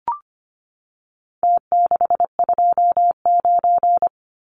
Morse code, heard as a keyed tone, about 25 wpm, 700 Hz. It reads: T629
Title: Numbers - Contest Style - 4-Digit 25wpm